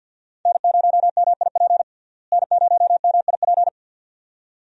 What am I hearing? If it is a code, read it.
Morse code, heard as a keyed tone, about 51 wpm, 700 Hz. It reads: N0MIP N0MIP
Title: USA Call Signs - 5 Characters - Repeated 2x - 50wpm